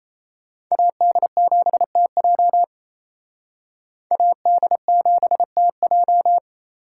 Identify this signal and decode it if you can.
Morse code, heard as a keyed tone, about 35 wpm, 700 Hz. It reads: AD7TJ AD7TJ
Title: Speed Racing - USA 2x2 Call Signs 28wpm